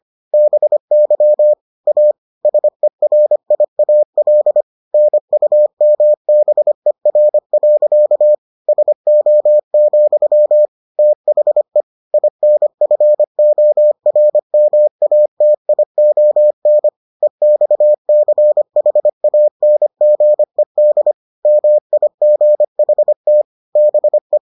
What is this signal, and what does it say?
Morse code, heard as a keyed tone, about 25 wpm, 600 Hz. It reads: BY A SERIAL NUMBER. SO, THE INFORMATION EXCHANGED MIGHT BE